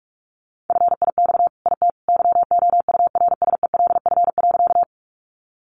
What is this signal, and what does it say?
Morse code, heard as a keyed tone, about 44 wpm, 700 Hz. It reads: FIX IT YOURSELF.
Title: Sentences from Top 1400 Words 45wpm